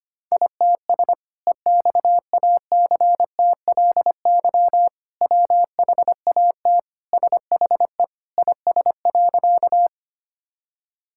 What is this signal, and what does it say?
Morse code, heard as a keyed tone, about 25 wpm, 700 Hz. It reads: ITS EXACTLY WHAT SHE IS.
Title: Sentences from Top 1000 Words - Review Part 6 - 25wpm